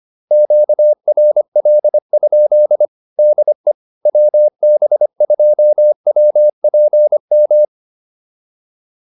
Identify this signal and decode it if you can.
Morse code, heard as a keyed tone, about 25 wpm, 600 Hz. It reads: QRL? DE WB2WPM
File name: QSO106 25